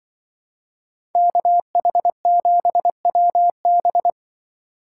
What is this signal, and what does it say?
Morse code, heard as a keyed tone, about 23 wpm, 700 Hz. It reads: KH7WB
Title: Speed Racing - USA Call Signs 20wpm